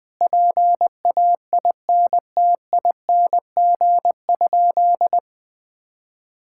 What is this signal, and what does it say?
Morse code, heard as a keyed tone, about 20 wpm, 700 Hz. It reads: PAINTING?
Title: Sentences from Top 1000 Words - Review Part 6 - 20wpm